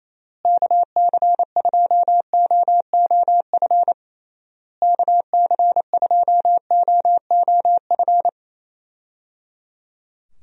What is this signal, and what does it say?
Morse code, heard as a keyed tone, about 28 wpm, 700 Hz. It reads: KC2OOF KC2OOF
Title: USA Call Signs - 6 Characters - Repeated 3x - 28wpm